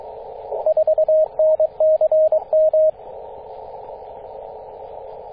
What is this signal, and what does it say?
Morse code, heard as a keyed tone, about 22 wpm, 630 Hz, with radio noise. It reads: 4NCM
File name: 9m0o-24-891mhz-cw